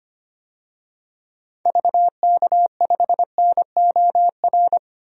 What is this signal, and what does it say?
Morse code, heard as a keyed tone, about 25 wpm, 700 Hz. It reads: VK5NOR
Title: DXCC-100 Call Signs - 6 Characters - Repeated 3x - 25wpm